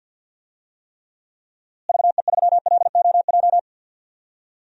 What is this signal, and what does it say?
Morse code, heard as a keyed tone, about 48 wpm, 700 Hz. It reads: VE3LOJ
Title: DXCC-100 Call Signs - 6 Characters - Repeated 3x - 50wpm